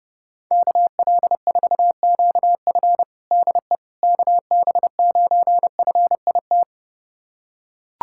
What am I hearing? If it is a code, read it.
Morse code, heard as a keyed tone, about 30 wpm, 700 Hz. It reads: KL4QF DE KB9FIT
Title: USA Callsign DE Callsign 30wpm